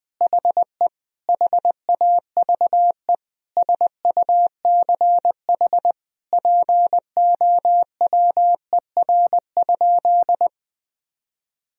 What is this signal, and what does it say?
Morse code, heard as a keyed tone, about 21 wpm, 700 Hz. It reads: HE HAVE SUCH POWER?
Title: Sentences from Top 1000 Words - Review Part 2 - 20wpm